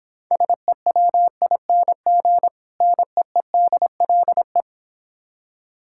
Morse code, heard as a keyed tone, about 26 wpm, 700 Hz. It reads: SEWING NEEDLE